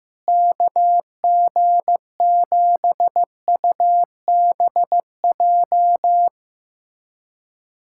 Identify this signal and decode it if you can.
Morse code, heard as a keyed tone, about 15 wpm, 700 Hz. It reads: KG7UBJ
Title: USA Callsign DE Callsign - 6 Characters - 15wpm